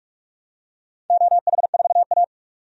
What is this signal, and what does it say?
Morse code, heard as a keyed tone, about 46 wpm, 700 Hz. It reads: OH4A
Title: DXCC-100 Call Signs - 4 Characters - 45wpm